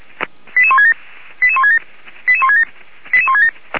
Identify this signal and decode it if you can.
Five-tone selective call (ZVEI), heard as quick runs of five tones, tones of about 80 ms. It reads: 88167 88167 88167 88167